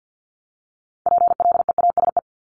Morse code, heard as a keyed tone, about 50 wpm, 690 Hz. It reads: PLEASE
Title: Multi-letter Phonemes Repeated 3x 50wpm